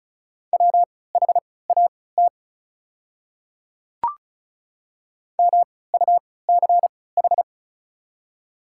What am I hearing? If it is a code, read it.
Morse code, heard as a keyed tone, about 36 wpm, spaced as for 20 wpm, 700 Hz. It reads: WHAT MUCH
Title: 3x Character Spacing - Top 200 Words - 35wpm